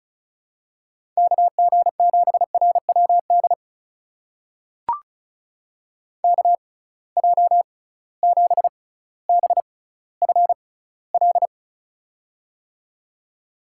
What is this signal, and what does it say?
Morse code, heard as a keyed tone, about 36 wpm, spaced as for 18 wpm, 700 Hz. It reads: KG7RWD KJ7BFL
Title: USA Call Signs - 6 Characters - 6x Character Spacing - 35wpm